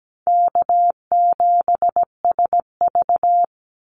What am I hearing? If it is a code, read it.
Morse code, heard as a keyed tone, about 17 wpm, 700 Hz. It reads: K7SV
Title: Easy Call Signs Repeated 2x - 17wpm